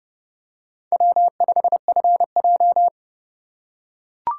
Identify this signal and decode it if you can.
Morse code, heard as a keyed tone, about 27 wpm, 700 Hz. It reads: W5FJ